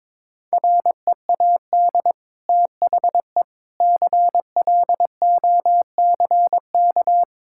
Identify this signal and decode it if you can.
Morse code, heard as a keyed tone, about 21 wpm, 700 Hz. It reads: READ THE CLOCK
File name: practice / 22wpm / Easy Sentences with Hints - 22wpm